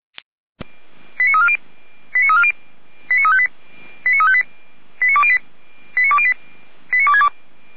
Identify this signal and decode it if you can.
Five-tone selective call (ZVEI), heard as quick runs of five tones, tones of about 70 ms. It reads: 89340 89358 89358 89208 89208 89262